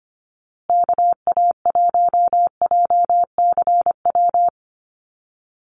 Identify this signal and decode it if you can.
Morse code, heard as a keyed tone, about 25 wpm, 700 Hz. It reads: KA1JCW
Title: USA Call Signs Repeated 2x 25wpm